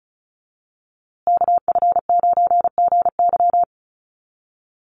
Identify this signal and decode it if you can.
Morse code, heard as a keyed tone, about 34 wpm, 700 Hz. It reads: KF9GY
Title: USA Call Signs Repeated 2x 35wpm